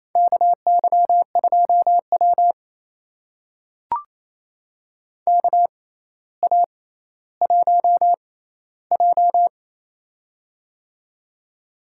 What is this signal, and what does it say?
Morse code, heard as a keyed tone, about 28 wpm, spaced as for 14 wpm, 700 Hz. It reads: KY2W KA1J
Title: USA Call Signs - 6x Character Spacing - 28wpm